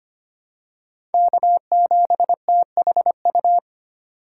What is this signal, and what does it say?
Morse code, heard as a keyed tone, about 25 wpm, 700 Hz. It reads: K7THU
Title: USA Call Signs - 5 Characters - 4x Character Spacing - 25wpm